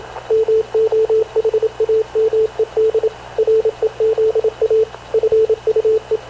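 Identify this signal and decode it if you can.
Morse code, heard as a keyed tone, about 26 wpm, 440 Hz, with radio noise. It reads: MOHAMED REZA FUE
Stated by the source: Havana Radio / CLA sending Spanish press (PX) in Morse, recorded by Radio Officer Paul Dumesnil onboard CNAV QUEST/CZDO.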